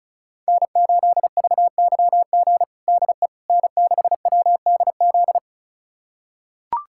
Morse code, heard as a keyed tone, about 36 wpm, 700 Hz. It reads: N8VYG DE N6WDZ
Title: USA Callsign DE Callsign - 5 Characters - 35wpm